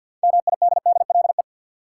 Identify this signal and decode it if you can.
Morse code, heard as a keyed tone, about 53 wpm, 700 Hz. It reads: MIDDLE
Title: Six Letter Words - 50wpm